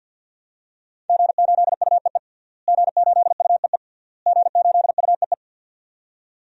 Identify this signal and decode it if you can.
Morse code, heard as a keyed tone, about 53 wpm, 700 Hz. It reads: G8UEE G8UEE G8UEE
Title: DXCC-100 Call Signs - 5 Characters - Repeated 3x - 50wpm